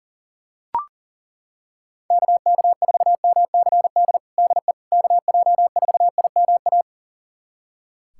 Morse code, heard as a keyed tone, about 40 wpm, 700 Hz. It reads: KK4MCD DE KJ4IMA
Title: USA Callsign DE Callsign - 6 Characters - 40wpm